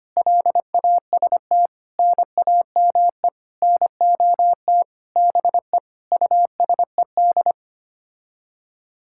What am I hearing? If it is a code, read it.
Morse code, heard as a keyed tone, about 25 wpm, 700 Hz. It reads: LAST NAME NOT BE USED
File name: Sets of 9 Words - 25wpm